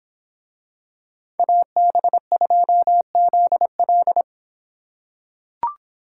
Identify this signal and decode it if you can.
Morse code, heard as a keyed tone, about 26 wpm, 700 Hz. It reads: AB2ZL